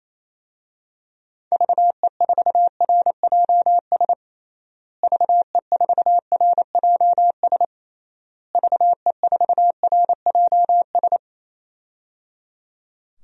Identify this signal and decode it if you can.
Morse code, heard as a keyed tone, about 28 wpm, 700 Hz. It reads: VE4RJS VE4RJS VE4RJS
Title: DXCC-100 Call Signs - 6 Characters - Repeated 3x - 28wpm